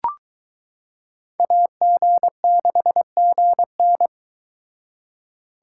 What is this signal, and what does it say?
Morse code, heard as a keyed tone, about 22 wpm, 700 Hz. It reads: AG6GN